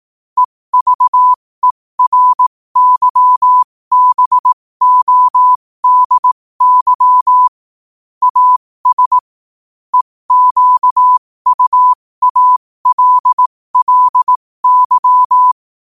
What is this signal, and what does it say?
Morse code, heard as a keyed tone, about 18 wpm, spaced as for 15 wpm, 1000 Hz. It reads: EVERYBODY AS EQUALLY